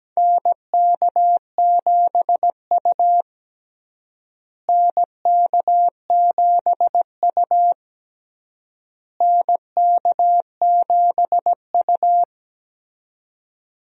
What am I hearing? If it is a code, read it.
Morse code, heard as a keyed tone, about 17 wpm, 700 Hz. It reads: NK7U NK7U NK7U
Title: USA Call Signs - 4 Characters - Repeated 3x - 17wpm